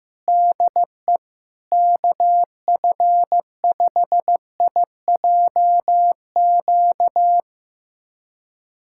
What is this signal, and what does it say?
Morse code, heard as a keyed tone, about 15 wpm, 700 Hz. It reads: DE KF5IJQ
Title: USA Callsign DE Callsign - 6 Characters - 15wpm